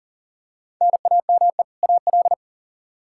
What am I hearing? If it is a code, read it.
Morse code, heard as a keyed tone, about 37 wpm, 700 Hz. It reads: NAME AL
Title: Practice QSO - Exchange 1 - Beginning - 40wpm